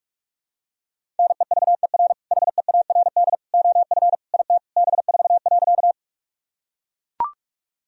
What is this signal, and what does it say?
Morse code, heard as a keyed tone, about 41 wpm, 700 Hz. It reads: NEVER HEARD OF IT B4.
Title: Sentences from Top 400 Words and Abbreviations 45wpm